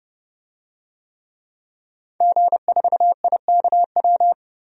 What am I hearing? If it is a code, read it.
Morse code, heard as a keyed tone, about 30 wpm, 700 Hz. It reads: G4IKW
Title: DXCC-100 Call Signs - 5 Characters - 30wpm